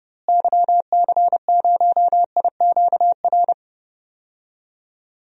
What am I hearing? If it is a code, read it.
Morse code, heard as a keyed tone, about 30 wpm, 700 Hz. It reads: YC0IQR
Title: DXCC-100 Call Signs - 6 Characters - Repeated 2x - 30wpm